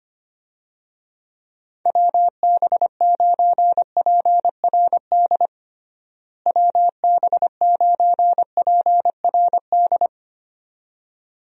Morse code, heard as a keyed tone, about 25 wpm, 700 Hz. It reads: WB9PRD WB9PRD
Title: USA Call Signs - 6 Characters - Repeated 2x - 25wpm